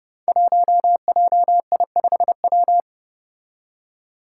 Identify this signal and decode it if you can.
Morse code, heard as a keyed tone, about 29 wpm, 700 Hz. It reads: 1JI5W
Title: Five Challenging Letter-Number Combinations 30wpm